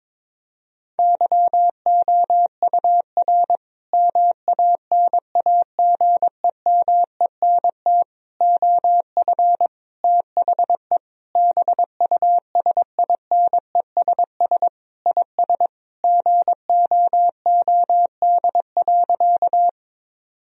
Morse code, heard as a keyed tone, about 21 wpm, 700 Hz. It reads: YOUR MANAGEMENT OF THE BUSINESS IS GOOD.
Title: Sentences from Top 1000 Words - Review Part 4 - 22wpm